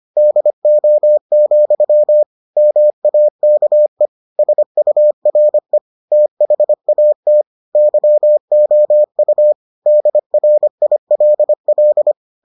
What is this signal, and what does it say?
Morse code, heard as a keyed tone, about 25 wpm, 600 Hz. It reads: DO, MAKE SURE THAT YOU DRILL